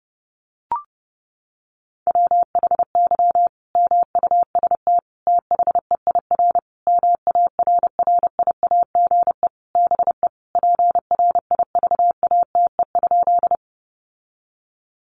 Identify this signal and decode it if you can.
Morse code, heard as a keyed tone, about 30 wpm, 700 Hz. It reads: WHY MUST THEIR MARRIAGE BE PRIVATE?
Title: Sentences from Top 1000 Words 30wpm